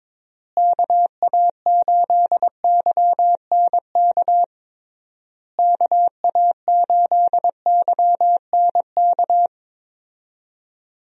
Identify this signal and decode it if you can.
Morse code, heard as a keyed tone, about 22 wpm, 700 Hz. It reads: KA8YNK KA8YNK
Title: USA Call Signs - 6 Characters - Repeated 3x - 22wpm